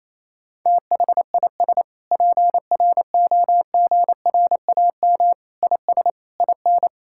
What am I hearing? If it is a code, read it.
Morse code, heard as a keyed tone, about 28 wpm, 700 Hz. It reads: THIS PROGRAM IS IN
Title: Sentences from Top 1000 Words - Review Part 5 - 28wpm